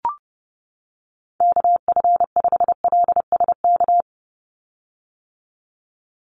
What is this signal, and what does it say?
Morse code, heard as a keyed tone, about 30 wpm, 700 Hz. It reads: KF5LSK
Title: USA Call Signs 30wpm